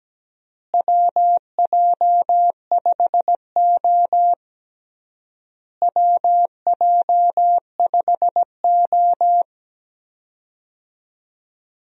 Morse code, heard as a keyed tone, about 17 wpm, 700 Hz. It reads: WJ5O WJ5O
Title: USA Call Signs - 4 Characters - Repeated 3x - 17wpm